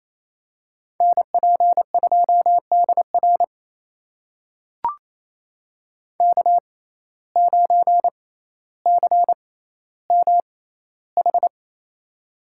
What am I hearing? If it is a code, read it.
Morse code, heard as a keyed tone, about 27 wpm, spaced as for 14 wpm, 700 Hz. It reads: NP2DR K9CMH
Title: USA Call Signs - 5 Characters - 6x Character Spacing - 28wpm